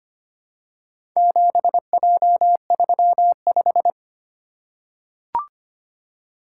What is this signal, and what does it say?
Morse code, heard as a keyed tone, about 25 wpm, 700 Hz. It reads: 7J35
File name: practice / 25wpm / Four Letter-Number Combinations - 25wpm